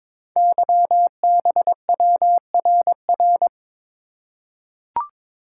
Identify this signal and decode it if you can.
Morse code, heard as a keyed tone, about 22 wpm, 700 Hz. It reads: YBWRR
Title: Code Groups - 5 Letters - 22wpm